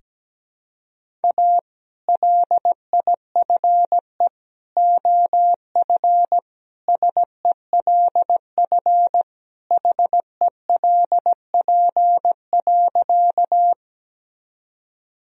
Morse code, heard as a keyed tone, about 17 wpm, 700 Hz. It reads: A LIFE OF SELF HELP.
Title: Sentences from Top 1000 Words - Review Part 1 - 17wpm